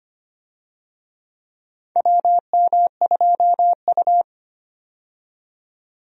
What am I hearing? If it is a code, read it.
Morse code, heard as a keyed tone, about 25 wpm, 700 Hz. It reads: WM2U
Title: USA Call Signs - 4 Characters - 25wpm